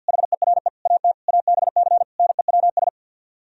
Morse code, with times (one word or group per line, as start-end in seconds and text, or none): 0.1-0.7 HERE
0.9-1.1 AT
1.3-2.0 ABC
2.2-2.9 NEWS